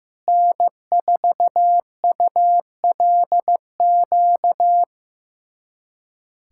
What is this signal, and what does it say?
Morse code, heard as a keyed tone, about 15 wpm, 700 Hz. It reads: N4ULQ
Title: USA Callsign DE Callsign - 5 Characters - 15wpm